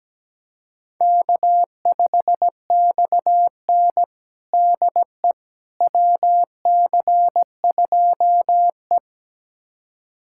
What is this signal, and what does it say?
Morse code, heard as a keyed tone, about 17 wpm, 700 Hz. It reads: K5XN DE WC2E
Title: USA Callsign DE Callsign - 4 Characters - 17wpm